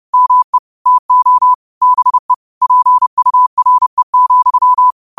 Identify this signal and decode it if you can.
Morse code, heard as a keyed tone, about 30 wpm, 1000 Hz. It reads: ME TO BE PURE,